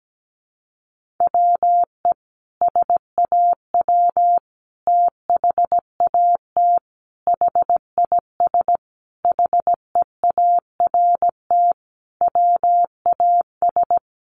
Morse code, heard as a keyed tone, about 17 wpm, 700 Hz. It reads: WE SAW THAT HIS HEART WAS
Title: Sentences from Top 1200 Words 17wpm